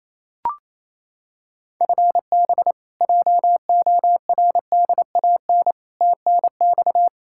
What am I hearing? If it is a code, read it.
Morse code, heard as a keyed tone, about 28 wpm, 700 Hz. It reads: FB JORDAN TNX
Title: Practice QSO - Exchange 3 - More Info - 28wpm